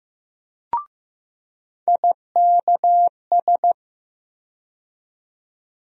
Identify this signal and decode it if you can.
Morse code, heard as a keyed tone, about 15 wpm, 700 Hz. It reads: IKS
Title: USA Call Sign Postfixes - 3 Letters 15wpm